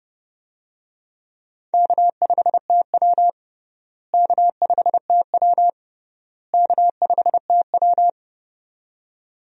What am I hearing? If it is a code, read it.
Morse code, heard as a keyed tone, about 30 wpm, 700 Hz. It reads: K5TW K5TW K5TW
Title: USA Call Signs - 4 Characters - Repeated 3x - 30wpm